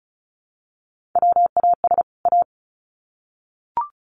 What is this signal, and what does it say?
Morse code, heard as a keyed tone, about 34 wpm, 700 Hz. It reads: WAS A
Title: Sets of 2 Words 35wpm